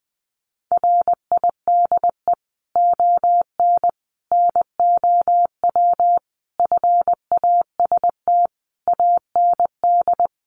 Morse code, heard as a keyed tone, about 19 wpm, 700 Hz. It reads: RIDE ON NOW FAST AND
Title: Sentences from Top 1500 Words 20wpm